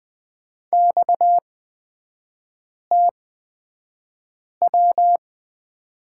Morse code, heard as a keyed tone, about 20 wpm, spaced as for 5 wpm, 700 Hz. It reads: XTW
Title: 243 Lesson 39e - Letter z - Three Characters 20wpm